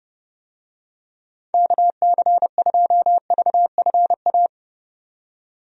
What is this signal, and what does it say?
Morse code, heard as a keyed tone, about 30 wpm, 700 Hz. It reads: KC2VFA
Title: USA Call Signs - 6 Characters - 30wpm